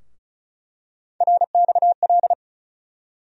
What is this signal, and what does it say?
Morse code, heard as a keyed tone, about 34 wpm, 700 Hz. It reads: RXL
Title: USA Call Sign Postfixes - 3 Letters 35wpm